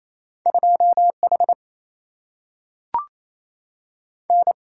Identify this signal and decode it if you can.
Morse code, heard as a keyed tone, about 28 wpm, 700 Hz. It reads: 2 H N
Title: Single-Letter Single-Number 28wpm